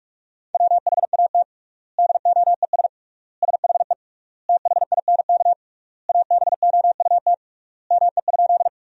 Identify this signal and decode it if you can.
Morse code, heard as a keyed tone, about 41 wpm, 700 Hz. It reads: WHAT DOES SHE THINK ABOUT ME?
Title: 3x Word-Spacing - Sentences from Top 200 Words - 45wpm